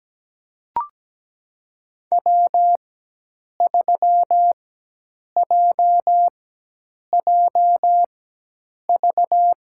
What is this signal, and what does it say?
Morse code, heard as a keyed tone, about 17 wpm, spaced as for 8 wpm, 700 Hz. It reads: W3JJV
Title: USA Call Signs - 5 Characters - 4x Character Spacing - 17wpm